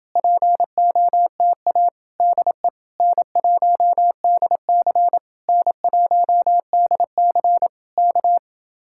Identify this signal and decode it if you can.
Morse code, heard as a keyed tone, about 29 wpm, 700 Hz. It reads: POTA DE N1DC N1DC K